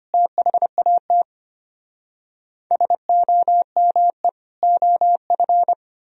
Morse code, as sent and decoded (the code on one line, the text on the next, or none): - .... .- - / ... --- -- . / --- ..-.
THAT SOME OF